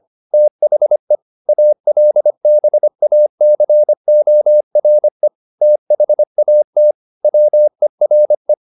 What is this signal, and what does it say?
Morse code, heard as a keyed tone, about 25 wpm, 600 Hz. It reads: THE ALBACORE THAT WERE
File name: the old man and the sea part 1 25wpm